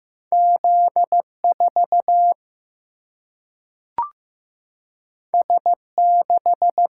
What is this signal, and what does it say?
Morse code, heard as a keyed tone, about 15 wpm, 700 Hz. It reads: Z 4 S 6
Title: Single-Letter Single-Number 15wpm